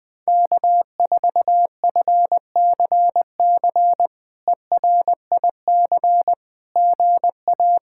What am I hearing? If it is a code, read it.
Morse code, heard as a keyed tone, about 20 wpm, 700 Hz. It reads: K4FCC ERIC GA
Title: K1USN SST QSO Practice - 20wpm